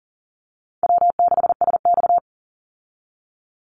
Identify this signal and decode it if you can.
Morse code, heard as a keyed tone, about 40 wpm, 700 Hz. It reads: W6SX
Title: Easy Call Signs Repeated 2x - 40wpm